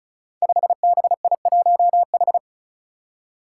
Morse code, heard as a keyed tone, about 36 wpm, 700 Hz. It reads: 5BI1H
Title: Five Challenging Letter-Number Combinations 35wpm